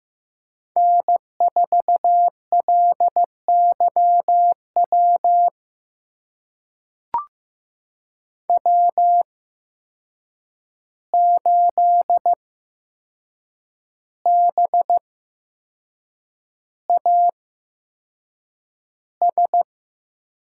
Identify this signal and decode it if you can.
Morse code, heard as a keyed tone, about 15 wpm, spaced as for 6 wpm, 700 Hz. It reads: N4LYW W8BAS
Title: USA Call Signs - 5 Characters - 8x Character Spacing - 15wpm